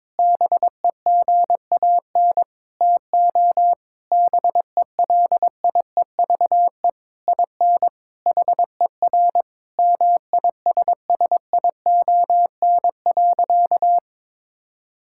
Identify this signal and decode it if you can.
Morse code, heard as a keyed tone, about 21 wpm, 700 Hz. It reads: BEGAN TO BELIEVE IN HER MISSION.
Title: Sentences from Top 1600 Words 22wpm